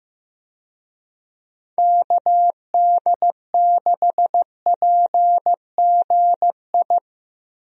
Morse code, heard as a keyed tone, about 15 wpm, 700 Hz. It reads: KD6PGI
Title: USA Call Signs - 6 Characters - Repeated 3x - 15wpm